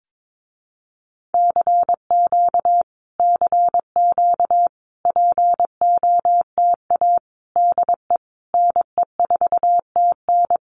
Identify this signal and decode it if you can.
Morse code, heard as a keyed tone, about 22 wpm, 700 Hz. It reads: CQ CQ POTA DE NE4TN
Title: Common QSO Phrases Encore 22wpm